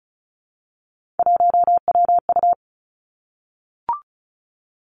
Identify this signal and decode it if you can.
Morse code, heard as a keyed tone, about 36 wpm, 700 Hz. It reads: 1WU
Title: Three Challenging Letter-Number Combinations 35wpm